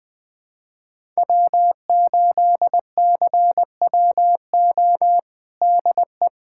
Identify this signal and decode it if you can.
Morse code, heard as a keyed tone, about 21 wpm, 700 Hz. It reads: W8CWO DE
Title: USA Callsign DE Callsign - 5 Characters - 20wpm